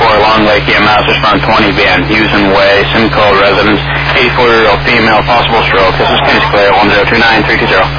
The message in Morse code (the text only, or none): none